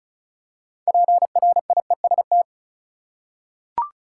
Morse code, heard as a keyed tone, about 36 wpm, 700 Hz. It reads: PRIEST
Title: Six Letter Words - 35wpm